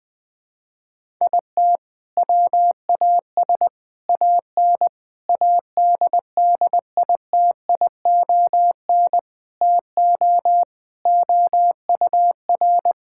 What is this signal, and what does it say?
Morse code, heard as a keyed tone, about 21 wpm, 700 Hz. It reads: IT WAS AN ADDITION TO OUR
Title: Sentences from Top 1000 Words - Review Part 6 - 20wpm